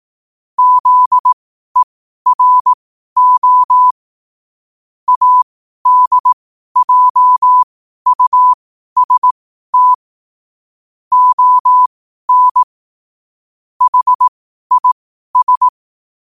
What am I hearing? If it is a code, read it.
Morse code, heard as a keyed tone, about 18 wpm, spaced as for 12 wpm, 1000 Hz. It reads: ZERO ADJUST ON HIS
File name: qotdfiles / 12wpm